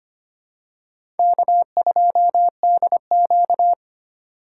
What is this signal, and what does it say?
Morse code, heard as a keyed tone, about 25 wpm, 700 Hz. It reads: K2DQ
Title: USA Call Signs - 2x Character Spacing - 25wpm